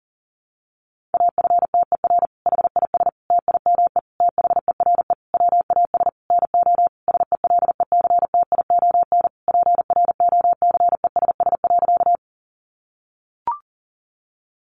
Morse code, heard as a keyed tone, about 40 wpm, 700 Hz. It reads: AFTER HIS TIME THERE WAS NO SELECTION PROCESS.
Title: Sentences from Top 1400 Words 40wpm